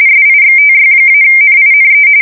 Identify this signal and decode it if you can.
RTTY, about 45.45 baud. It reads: SIXTY-NINE
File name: fsk170 / USS-New-Jersey